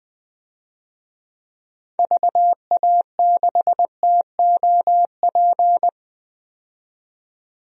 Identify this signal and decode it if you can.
Morse code, heard as a keyed tone, about 20 wpm, 700 Hz. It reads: VA6TOP
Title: DXCC-100 Call Signs - 6 Characters - 20wpm